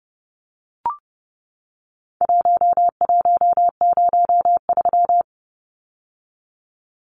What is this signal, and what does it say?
Morse code, heard as a keyed tone, about 30 wpm, 700 Hz. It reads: 1103
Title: Numbers - Contest Style - 4-Digit 30wpm